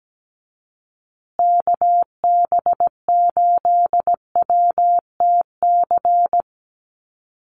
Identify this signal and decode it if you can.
Morse code, heard as a keyed tone, about 17 wpm, 700 Hz. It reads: KB8WTC
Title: USA Call Signs Repeated 2x 17wpm